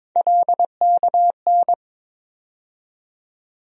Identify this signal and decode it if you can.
Morse code, heard as a keyed tone, about 22 wpm, 700 Hz. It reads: LKN
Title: USA Call Sign Postfixes - 3 Letters 22wpm